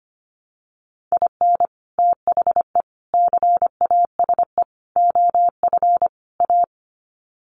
Morse code, heard as a keyed tone, about 25 wpm, 700 Hz. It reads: IN THE CASE OF A
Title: Sets of 5 Words Encore 25wpm